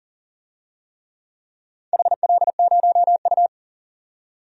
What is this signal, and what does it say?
Morse code, heard as a keyed tone, about 40 wpm, 700 Hz. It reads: HL0U
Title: DXCC-100 Call Signs - 4 Characters - 40wpm